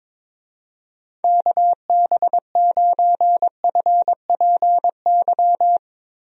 Morse code, heard as a keyed tone, about 22 wpm, 700 Hz. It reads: KB9FPY
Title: USA Call Signs - 6 Characters - Repeated 3x - 22wpm